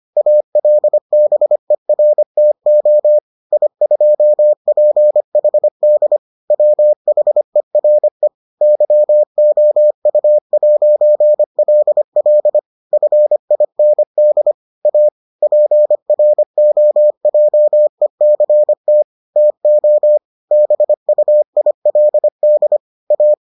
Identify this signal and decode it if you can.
Morse code, heard as a keyed tone, about 25 wpm, 600 Hz. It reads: ALBERTO I2PHD WHERE YOU'LL FIND A PROJECT TO BUILD A